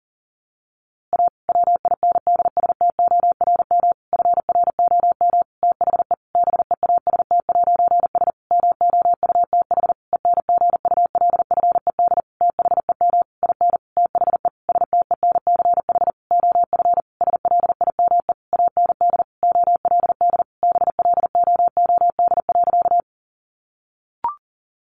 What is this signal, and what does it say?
Morse code, heard as a keyed tone, about 40 wpm, 700 Hz. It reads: A WINDSTORM FROM THE BEAST'S MOUTH ENGULFED THEM IN THE STENCH OF SLIME AND OLD BLOOD.
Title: A Sound Of Thunder 40wpm